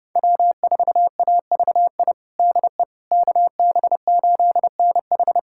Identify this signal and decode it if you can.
Morse code, heard as a keyed tone, about 32 wpm, 700 Hz. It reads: W4AVI DE KB8NH
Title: USA Callsign DE Callsign - 5 Characters - 30wpm